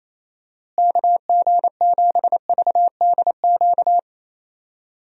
Morse code, heard as a keyed tone, about 28 wpm, 700 Hz. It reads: KG7VDQ
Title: USA Call Signs - 6 Characters - Repeated 2x - 28wpm